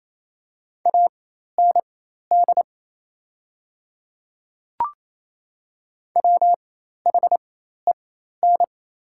Morse code, heard as a keyed tone, about 28 wpm, spaced as for 13 wpm, 700 Hz. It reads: AND WHEN
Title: 4x Character Spacing - Top 300 Words - 28wpm